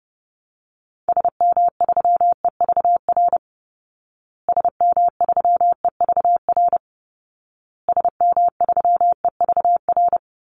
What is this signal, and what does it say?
Morse code, heard as a keyed tone, about 30 wpm, 700 Hz. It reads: SM3EVR SM3EVR SM3EVR
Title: Call Signs Repeated 2x - 30wpm